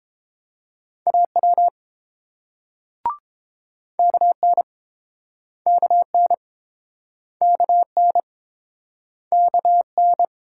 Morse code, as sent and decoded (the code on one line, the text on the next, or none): .- .-- / -.- -. / -.- -. / -.- -. / -.- -.
AW KN KN KN KN